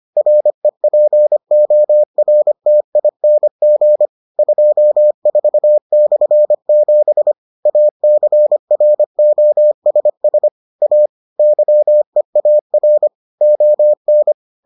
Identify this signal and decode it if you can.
Morse code, heard as a keyed tone, about 25 wpm, 600 Hz. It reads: REPORTING 24/7 ACROSS A YEAR ON